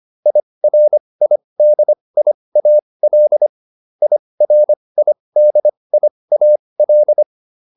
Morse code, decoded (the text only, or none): IRIDIAL IRIDIAL